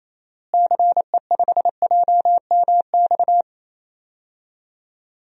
Morse code, heard as a keyed tone, about 28 wpm, 700 Hz. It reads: CE5JMX
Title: DXCC-100 Call Signs - 6 Characters - Repeated 2x - 28wpm